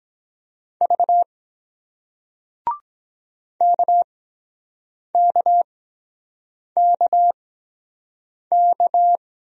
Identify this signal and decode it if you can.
Morse code, heard as a keyed tone, about 23 wpm, spaced as for 7 wpm, 700 Hz. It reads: V KKKK